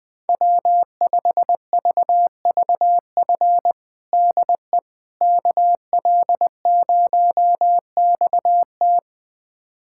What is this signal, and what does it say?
Morse code, heard as a keyed tone, about 21 wpm, 700 Hz. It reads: W5VVF DE KL0XT
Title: USA Callsign DE Callsign - 5 Characters - 20wpm